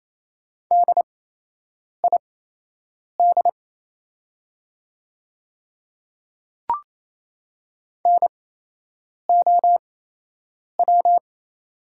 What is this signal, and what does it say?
Morse code, heard as a keyed tone, about 28 wpm, spaced as for 8 wpm, 700 Hz. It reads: DID NOW
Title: 8x Character Spacing - Top 500 Words - 28wpm